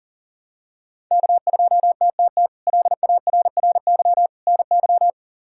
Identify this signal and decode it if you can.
Morse code, heard as a keyed tone, about 40 wpm, 700 Hz. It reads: K2TTT LARRY NY
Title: K1USN SST QSO Practice - 40wpm